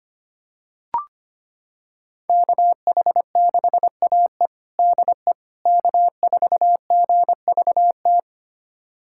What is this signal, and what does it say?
Morse code, heard as a keyed tone, about 25 wpm, 700 Hz. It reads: KH6AE DE K4GVT
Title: USA Callsign DE Callsign - 5 Characters - 25wpm